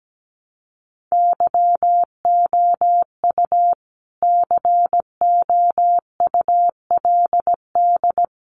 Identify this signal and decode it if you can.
Morse code, heard as a keyed tone, about 17 wpm, 700 Hz. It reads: YOU COULD
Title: Sets of 2 Words 17wpm